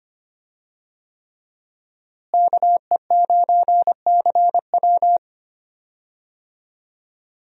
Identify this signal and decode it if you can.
Morse code, heard as a keyed tone, about 25 wpm, 700 Hz. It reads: KE9CW
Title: USA Call Signs - 5 Characters - Repeated 3x - 25wpm